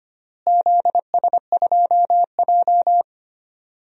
Morse code, heard as a keyed tone, about 25 wpm, 700 Hz. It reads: ZS2J
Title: DXCC-100 Call Signs - 4 Characters - Repeated 3x - 25wpm